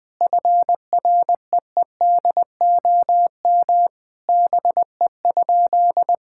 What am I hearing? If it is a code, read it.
Morse code, heard as a keyed tone, about 20 wpm, 700 Hz. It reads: FREEDOM BE?
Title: Sentences from Top 1000 Words - Review Part 5 - 20wpm